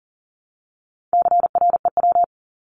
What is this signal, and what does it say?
Morse code, heard as a keyed tone, about 40 wpm, 700 Hz. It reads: CREW
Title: Top 1251-1500 Words 40wpm